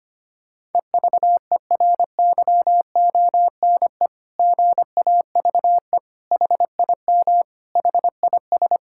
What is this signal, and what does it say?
Morse code, heard as a keyed tone, about 25 wpm, 700 Hz. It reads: EVERYONE GAVE HIM HIS
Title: Sentences from Top 1000 Words - Review Part 6 - 25wpm